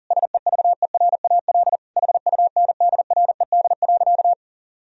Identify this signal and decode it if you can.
Morse code, heard as a keyed tone, about 40 wpm, 700 Hz. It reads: SEVERAL HUNDRED.
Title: Sentences from Top 1000 Words - Review Part 2 - 40wpm